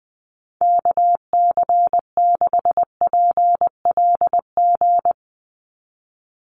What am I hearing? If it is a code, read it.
Morse code, heard as a keyed tone, about 19 wpm, 700 Hz. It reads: KC6PLG
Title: USA Call Signs 20wpm